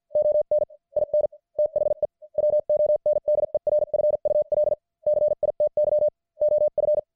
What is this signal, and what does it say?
Morse code, heard as a keyed tone, about 50 wpm, spaced as for 44 wpm, 600 Hz. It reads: ON IN THE WONDERFUL CITY OF
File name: archived / CWBros